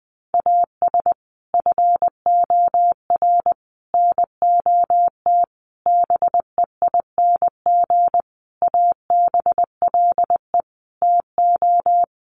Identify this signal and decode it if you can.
Morse code, heard as a keyed tone, about 20 wpm, 700 Hz. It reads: AS FOR NOT BEING ABLE TO
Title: Sentences from Top 1100 Words 20wpm